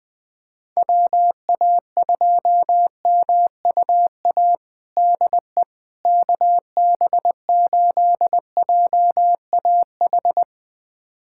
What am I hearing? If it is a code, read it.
Morse code, heard as a keyed tone, about 21 wpm, 700 Hz. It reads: WA2MUA DE KB8JAH
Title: USA Callsign DE Callsign 20wpm